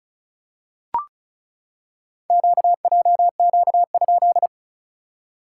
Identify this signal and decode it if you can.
Morse code, heard as a keyed tone, about 36 wpm, 700 Hz. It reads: QJQ?
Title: SOTA QSO Practice 35wpm